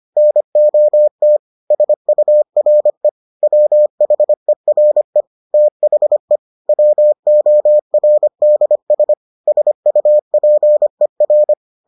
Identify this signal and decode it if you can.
Morse code, heard as a keyed tone, about 25 wpm, 600 Hz. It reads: NOT SURE WHERE THE WORDS SUPER